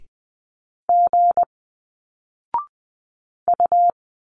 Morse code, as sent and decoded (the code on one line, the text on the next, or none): --. / ..-
G U